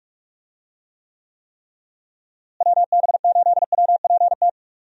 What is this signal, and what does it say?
Morse code, heard as a keyed tone, about 41 wpm, 700 Hz. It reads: WB8WPT
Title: USA Call Signs - 6 Characters - Repeated 3x - 45wpm